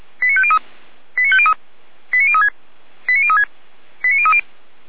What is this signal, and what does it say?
Five-tone selective call (ZVEI), heard as quick runs of five tones, tones of about 70 ms. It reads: 89503 89503 89036 89036 89030